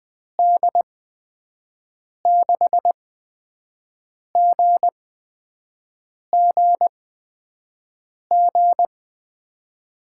Morse code, heard as a keyed tone, about 20 wpm, spaced as for 5 wpm, 700 Hz. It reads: D6GGG
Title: USA Call Signs - 6 Characters - 8x Character Spacing - 20wpm